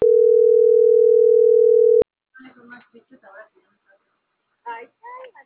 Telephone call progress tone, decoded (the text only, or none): ringback tone